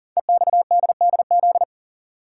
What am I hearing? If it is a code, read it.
Morse code, heard as a keyed tone, about 38 wpm, 700 Hz. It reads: EXDDZ
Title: Code Groups - 5 Letters - 40wpm